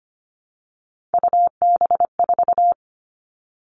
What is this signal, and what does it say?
Morse code, heard as a keyed tone, about 25 wpm, 700 Hz. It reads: UB4
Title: Three Challenging Letter-Number Combinations 25wpm